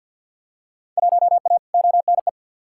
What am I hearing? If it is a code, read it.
Morse code, heard as a keyed tone, about 48 wpm, 700 Hz. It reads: 1A ONE
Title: Field Day Exchanges - 50wpm